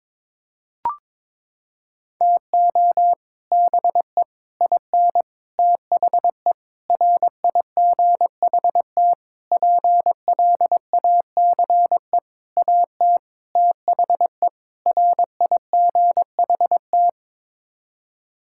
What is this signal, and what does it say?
Morse code, heard as a keyed tone, about 22 wpm, 700 Hz. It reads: TO BE IN THE RIGHT PLACE AT THE RIGHT
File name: Sets of 9 Words - 22wpm